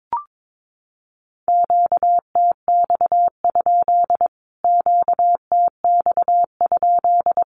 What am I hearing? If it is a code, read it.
Morse code, heard as a keyed tone, about 22 wpm, 700 Hz. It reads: QTX? QTX?
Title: Q-Codes Repeated 3x 22wpm